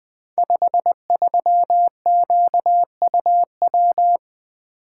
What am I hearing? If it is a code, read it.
Morse code, heard as a keyed tone, about 20 wpm, 700 Hz. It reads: 53QUW
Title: Code Groups - 5 Letter:Numbers - 20wpm